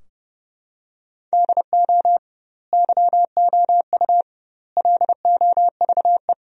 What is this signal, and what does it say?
Morse code, heard as a keyed tone, about 30 wpm, 700 Hz. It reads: DO YOU LOVE
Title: 2x Word-Spacing - Sentences from Top 400 Words - 30wpm